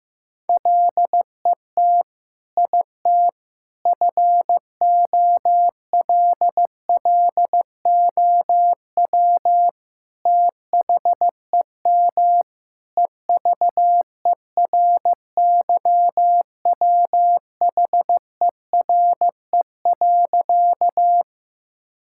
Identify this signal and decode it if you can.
Morse code, heard as a keyed tone, about 15 wpm, 700 Hz. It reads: LET IT FOLLOW THEM EVERYWHERE.
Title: Sentences from Top 1600 Words 15wpm